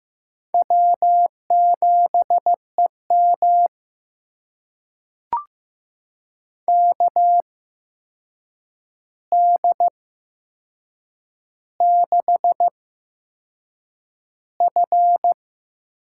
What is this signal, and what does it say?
Morse code, heard as a keyed tone, about 15 wpm, spaced as for 6 wpm, 700 Hz. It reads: W7EM KD6F
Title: USA Call Signs - 4 Characters - 8x Character Spacing - 15wpm